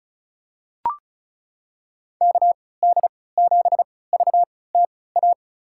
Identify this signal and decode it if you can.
Morse code, heard as a keyed tone, about 36 wpm, spaced as for 20 wpm, 700 Hz. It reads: KD7VTA
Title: USA Call Signs - 6 Characters - 3x Character Spacing - 35wpm